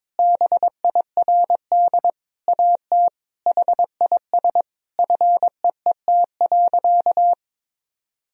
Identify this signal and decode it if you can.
Morse code, heard as a keyed tone, about 22 wpm, 700 Hz. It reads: BIRD AT HIS FEET.
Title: Sentences from Top 1000 Words - Review Part 4 - 22wpm